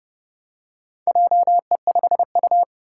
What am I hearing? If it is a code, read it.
Morse code, heard as a keyed tone, about 30 wpm, 700 Hz. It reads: JE5U
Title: Four Challenging Letter-Number Combinations 30wpm